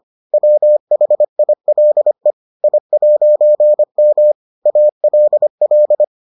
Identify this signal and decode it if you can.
Morse code, heard as a keyed tone, about 25 wpm, 600 Hz. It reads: WHILE I'M ALL